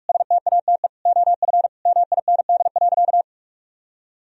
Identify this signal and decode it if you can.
Morse code, heard as a keyed tone, about 46 wpm, 700 Hz. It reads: STATE OF MIND.
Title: Sentences from Top 1000 Words - Review Part 2 - 45wpm